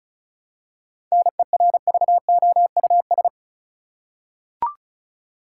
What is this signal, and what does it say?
Morse code, heard as a keyed tone, about 34 wpm, 700 Hz. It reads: NERVOUS
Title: Seven Letter Words - 35wpm